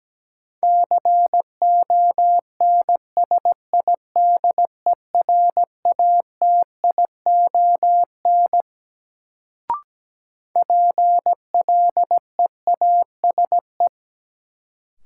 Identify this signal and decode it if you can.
Morse code, heard as a keyed tone, about 17 wpm, 700 Hz. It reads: CONSIDERATION PLEASE
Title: Sentences from Top 1500 Words - New Word Drill - 17wpm